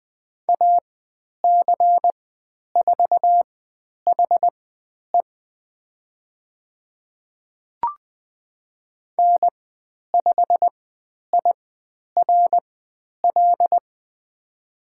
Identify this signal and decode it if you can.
Morse code, heard as a keyed tone, about 20 wpm, spaced as for 10 wpm, 700 Hz. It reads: AC4HE N5IRL
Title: 102 Lesson 17g - Letter u - Call Signs 20wpm